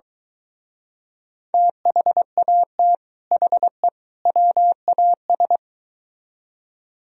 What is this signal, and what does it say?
Morse code, heard as a keyed tone, about 23 wpm, 700 Hz. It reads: THAT HE WAS